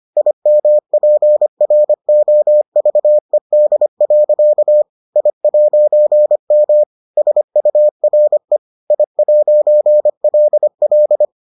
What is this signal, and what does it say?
Morse code, heard as a keyed tone, about 25 wpm, 600 Hz. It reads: IMPROVED. I'M SURE I'LL